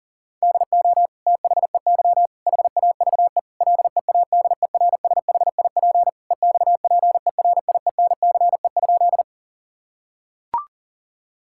Morse code, heard as a keyed tone, about 40 wpm, 700 Hz. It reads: DO THEY HAVE LEADERSHIP EXPERIENCE?
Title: Sentences from Top 1600 Words 40wpm